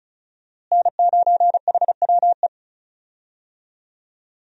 Morse code, heard as a keyed tone, about 36 wpm, 700 Hz. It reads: N9HWE
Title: USA Call Signs - 5 Characters - Repeated 2x - 35wpm